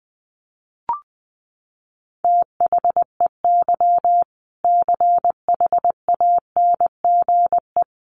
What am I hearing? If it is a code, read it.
Morse code, heard as a keyed tone, about 20 wpm, 700 Hz. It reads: THEY CHANGE
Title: Sentences from Top 1200 Words 20wpm